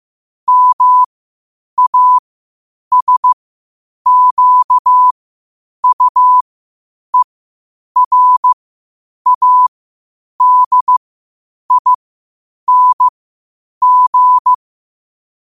Morse code, heard as a keyed tone, about 15 wpm, spaced as for 8 wpm, 1000 Hz. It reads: MASQUERADING
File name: qotdfiles / 8wpm